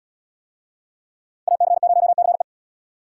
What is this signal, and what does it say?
Morse code, heard as a keyed tone, about 77 wpm, 700 Hz. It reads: AC0CE